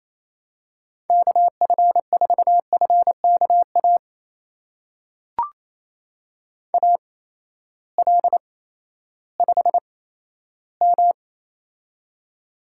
Morse code, heard as a keyed tone, about 27 wpm, spaced as for 14 wpm, 700 Hz. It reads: KF4FKA AL5M
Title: USA Call Signs - 8x Character Spacing - 28wpm